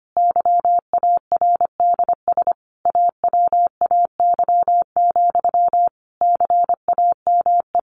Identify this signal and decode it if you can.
Morse code, heard as a keyed tone, about 25 wpm, 700 Hz. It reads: YARDS AWAY, CAME
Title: A Sound Of Thunder 25wpm